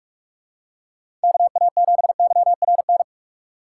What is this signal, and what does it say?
Morse code, heard as a keyed tone, about 44 wpm, 700 Hz. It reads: KA7YRN